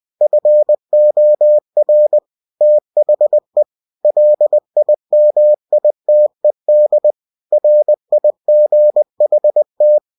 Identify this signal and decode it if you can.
Morse code, heard as a keyed tone, about 20 wpm, 600 Hz. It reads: FOR THE LIMITED RIGHT
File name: cw-books / winnie-the-pooh / winnie-20wpm-page-0119